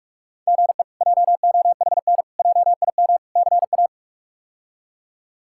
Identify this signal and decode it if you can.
Morse code, heard as a keyed tone, about 46 wpm, 700 Hz. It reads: GE JOHN JIM CA
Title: K1USN SST QSO Practice - 45wpm